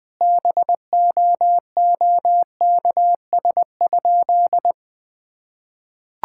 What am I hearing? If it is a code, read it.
Morse code, heard as a keyed tone, about 20 wpm, 700 Hz. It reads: BOOKS?
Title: Sentences with 9-Word Phrases - 20wpm